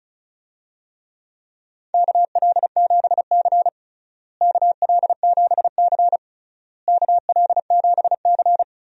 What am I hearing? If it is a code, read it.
Morse code, heard as a keyed tone, about 34 wpm, 700 Hz. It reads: KL7C KL7C KL7C
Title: DXCC-100 Call Signs - 4 Characters - Repeated 3x - 35wpm